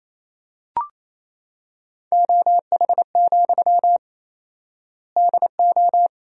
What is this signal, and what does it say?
Morse code, heard as a keyed tone, about 28 wpm, 700 Hz. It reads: OH, DO
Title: 4x Word-Spacing - Sentences from Top 500 Words - 28wpm